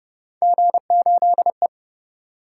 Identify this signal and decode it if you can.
Morse code, heard as a keyed tone, about 33 wpm, 700 Hz. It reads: G8E